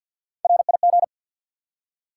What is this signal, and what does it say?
Morse code, heard as a keyed tone, about 48 wpm, 700 Hz. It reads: RIG
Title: Top 90 QSO Elements 50wpm